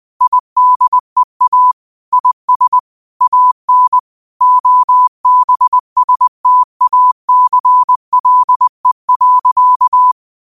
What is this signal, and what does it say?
Morse code, heard as a keyed tone, about 20 wpm, 1000 Hz. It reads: IDEA IS AN OBSTACLE.
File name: qotdfiles / 20wpm